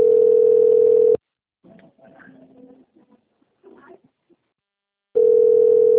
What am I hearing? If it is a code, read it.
Ringback tone (North America), 440 + 480 Hz.
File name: rg-1600-unknown- - - .55886